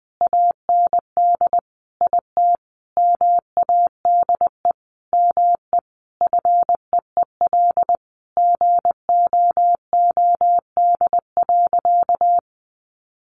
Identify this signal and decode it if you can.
Morse code, heard as a keyed tone, about 20 wpm, 700 Hz. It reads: AND IT MADE ME FEEL GOOD.
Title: Sentences from Top 600 Words 20wpm